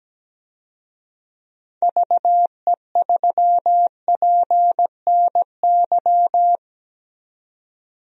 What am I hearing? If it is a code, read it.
Morse code, heard as a keyed tone, about 17 wpm, 700 Hz. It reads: VE3PNY
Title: DXCC-100 Call Signs - 6 Characters - 17wpm